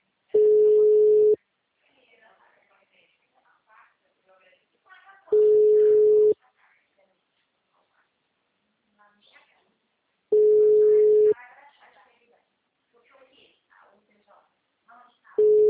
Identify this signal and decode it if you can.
Ringback tone (Europe, 425 Hz), with speech.